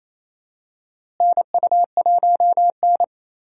Morse code, heard as a keyed tone, about 28 wpm, 700 Hz. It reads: NU1N
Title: USA Call Signs - 4 Characters - Repeated 3x - 28wpm